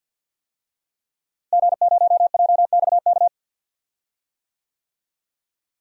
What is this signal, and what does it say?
Morse code, heard as a keyed tone, about 52 wpm, 700 Hz. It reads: G0JXK
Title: DXCC-100 Call Signs - 5 Characters - Repeated 2x - 50wpm